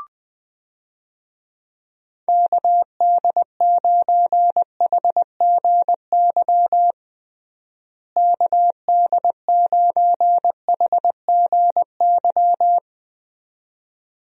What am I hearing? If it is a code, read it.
Morse code, heard as a keyed tone, about 20 wpm, 700 Hz. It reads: KD9HGY KD9HGY
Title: USA Call Signs - 6 Characters - Repeated 2x - 20wpm